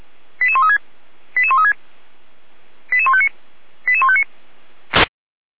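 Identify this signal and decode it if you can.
Five-tone selective call (ZVEI), heard as quick runs of five tones, tones of about 70 ms. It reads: 88136 88136 88159 88159
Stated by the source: ZVEI five-tone call received on SDR, 455 channel, logged by feuerwehr-orlamuende.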